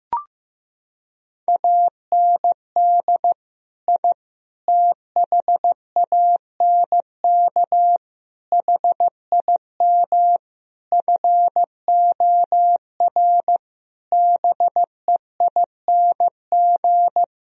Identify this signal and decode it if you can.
Morse code, heard as a keyed tone, about 15 wpm, 700 Hz. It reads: AND I THANK HIM FOR BEING
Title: Sentences from Top 1000 Words - Review Part 3 - 15wpm